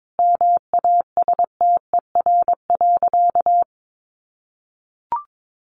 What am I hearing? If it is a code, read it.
Morse code, heard as a keyed tone, about 22 wpm, 700 Hz. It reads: MASTER.
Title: Sentences from Top 1300 Words 22wpm